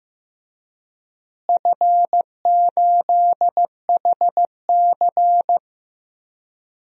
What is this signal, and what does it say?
Morse code, heard as a keyed tone, about 15 wpm, 700 Hz. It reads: F8HC
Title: DXCC-100 Call Signs - 4 Characters - Repeated 3x - 15wpm